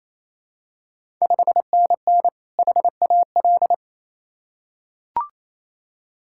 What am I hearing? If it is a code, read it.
Morse code, heard as a keyed tone, about 28 wpm, 700 Hz. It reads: 5NN HAL
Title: Ontario QSO Party Exchange 28wpm